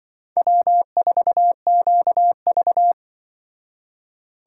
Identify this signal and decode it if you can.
Morse code, heard as a keyed tone, about 25 wpm, 700 Hz. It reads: W4QV
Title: Speed Racing - USA 1x2 Call Signs 20wpm